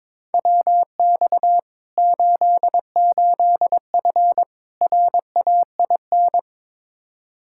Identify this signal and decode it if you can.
Morse code, heard as a keyed tone, about 22 wpm, 700 Hz. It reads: WX 88F RAIN
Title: Practice QSO - Exchange 2 - Info - 22wpm